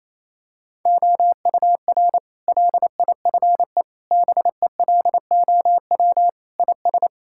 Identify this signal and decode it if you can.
Morse code, heard as a keyed tone, about 28 wpm, 700 Hz. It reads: OUR LIFE BELOW IS
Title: Sentences from Top 1000 Words - Review Part 5 - 28wpm